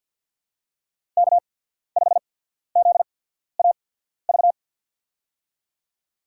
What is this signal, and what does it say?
Morse code, heard as a keyed tone, about 53 wpm, spaced as for 14 wpm, 700 Hz. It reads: K5ZAV
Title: USA Call Signs - 5 Characters - 8x Character Spacing - 50wpm